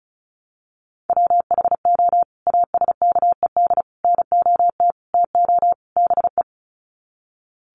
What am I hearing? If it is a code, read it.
Morse code, heard as a keyed tone, about 34 wpm, 700 Hz. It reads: WHO ASKED NOT TO BE
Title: Sets of 5 Words Encore2 35wpm